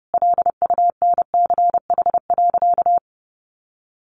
Morse code, heard as a keyed tone, about 30 wpm, 700 Hz. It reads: LUNCH.
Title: Easy Phrases 30wpm